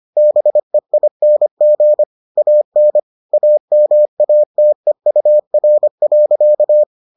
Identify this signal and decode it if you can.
Morse code, heard as a keyed tone, about 25 wpm, 600 Hz. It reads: BEING AN AMATEUR.